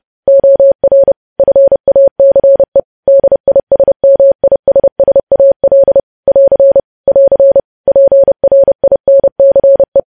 Morse code, heard as a keyed tone, about 30 wpm, 550 Hz. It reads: OR FACE DISMISSAL <AR> <AR> PRINCE